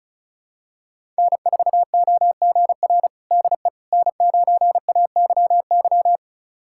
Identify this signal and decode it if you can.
Morse code, heard as a keyed tone, about 36 wpm, 700 Hz. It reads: N4OGR DE N9AYY
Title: USA Callsign DE Callsign - 5 Characters - 35wpm